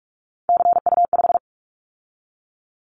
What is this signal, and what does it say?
Morse code, heard as a keyed tone, about 44 wpm, 700 Hz. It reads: CU5
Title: Callsign Prefixes for Top 50 DXCC - 45wpm